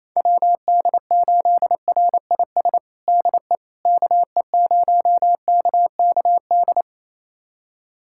Morse code, heard as a keyed tone, about 28 wpm, 700 Hz. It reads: WD8RIS DE KE0KKD
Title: USA Callsign DE Callsign - 6 Characters - 28wpm